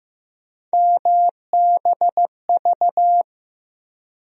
Morse code, heard as a keyed tone, about 15 wpm, 700 Hz. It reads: MBV
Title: USA Call Sign Postfixes - 3 Letters 15wpm